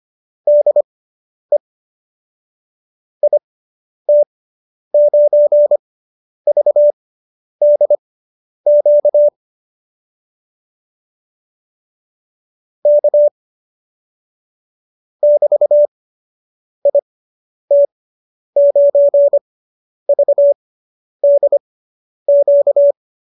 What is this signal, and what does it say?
Morse code, heard as a keyed tone, about 25 wpm, spaced as for 10 wpm, 590 Hz. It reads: DE IT9VDQ K <BT>IT9VDQ